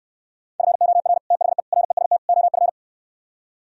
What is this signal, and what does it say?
Morse code, heard as a keyed tone, about 69 wpm, 700 Hz. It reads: FOR THE REST OF